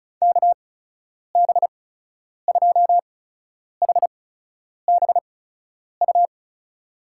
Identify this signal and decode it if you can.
Morse code, heard as a keyed tone, about 36 wpm, spaced as for 10 wpm, 700 Hz. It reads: KB2HBU
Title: USA Call Signs - 6 Characters - 8x Character Spacing - 35wpm